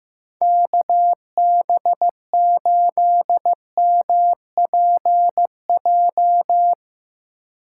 Morse code, heard as a keyed tone, about 15 wpm, 700 Hz. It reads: KB8MPJ
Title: Speed Racing - USA 2x3 Call Signs 15wpm